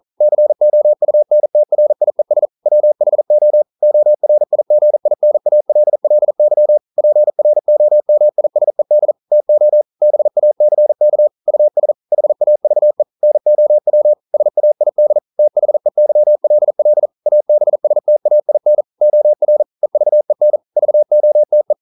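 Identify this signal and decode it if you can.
Morse code, heard as a keyed tone, about 42 wpm, 600 Hz. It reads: COUNTRIES WHO ORIGINALLY PROMISED TO BACK US HAVE NOW SAID THEYLL ABSTAIN OR EVEN VOTE
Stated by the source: Morse code practice at 41 wpm, from CW med Gnister.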